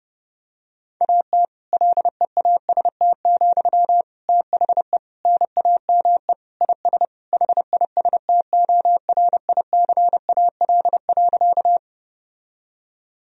Morse code, heard as a keyed tone, about 30 wpm, 700 Hz. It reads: AT LEAST, THE NAME IS HISTORICAL.
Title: Sentences from Top 1000 Words - Review Part 4 - 30wpm